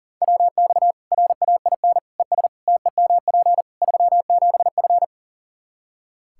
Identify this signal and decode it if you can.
Morse code, heard as a keyed tone, about 37 wpm, 700 Hz. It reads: WX RAIN ES TEMP 37F
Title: Practice QSO - Exchange 2 - Info - 40wpm